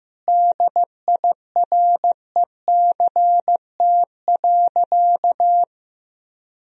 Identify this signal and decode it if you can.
Morse code, heard as a keyed tone, about 16 wpm, 700 Hz. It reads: DIRECT.
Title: Sentences from Top 1000 Words - Review Part 2 - 15wpm